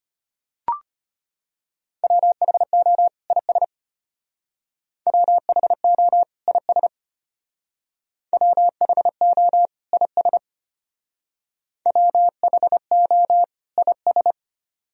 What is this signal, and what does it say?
Morse code, heard as a keyed tone, about 32 wpm, 700 Hz. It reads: WHO IS WHO IS WHO IS WHO IS